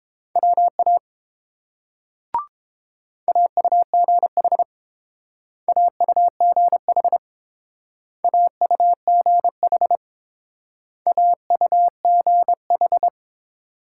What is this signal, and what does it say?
Morse code, heard as a keyed tone, about 30 wpm, 700 Hz. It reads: WA AUGH AUGH AUGH AUGH